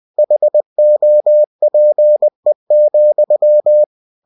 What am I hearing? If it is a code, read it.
Morse code, heard as a keyed tone, about 20 wpm, 600 Hz. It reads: HOPE,
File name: cw-books / winnie-the-pooh / winnie-20wpm-page-0101